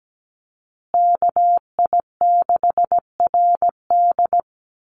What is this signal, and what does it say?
Morse code, heard as a keyed tone, about 17 wpm, 700 Hz. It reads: KI6RD
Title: USA Call Signs Repeated 2x 17wpm